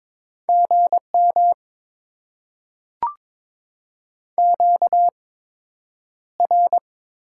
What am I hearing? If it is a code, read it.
Morse code, heard as a keyed tone, about 22 wpm, 700 Hz. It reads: G M Q R
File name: Top 20 QSO Elements - 8x Character Spacing - 22wpm